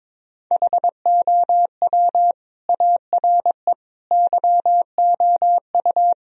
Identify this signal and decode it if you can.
Morse code, heard as a keyed tone, about 21 wpm, 700 Hz. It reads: HOW ARE YOU
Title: Sentences from Top 1000 Words - Review Part 4 - 22wpm